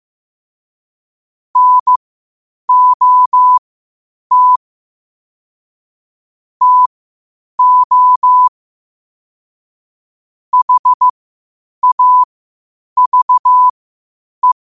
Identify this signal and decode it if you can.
Morse code, heard as a keyed tone, about 15 wpm, spaced as for 8 wpm, 1000 Hz. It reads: NOT TO HAVE
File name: qotdfiles / 8wpm